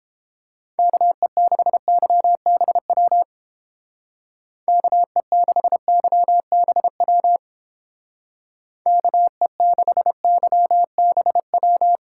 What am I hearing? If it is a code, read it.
Morse code, heard as a keyed tone, about 30 wpm, 700 Hz. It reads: KE6YBW KE6YBW KE6YBW